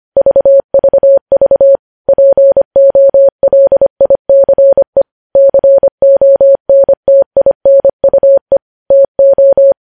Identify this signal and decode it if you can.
Morse code, heard as a keyed tone, about 25 wpm, 550 Hz. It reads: VVV POLICE CONTINUE TO
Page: news-headlines-in-morse-code-at-25-wpm